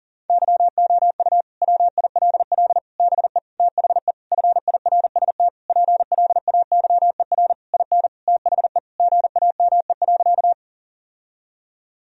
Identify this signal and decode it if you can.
Morse code, heard as a keyed tone, about 40 wpm, 700 Hz. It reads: YOU WILL BE THE FIRST PLAYER IN THE GAME.
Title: Sentences from Top 1000 Words - Review Part 4 - 40wpm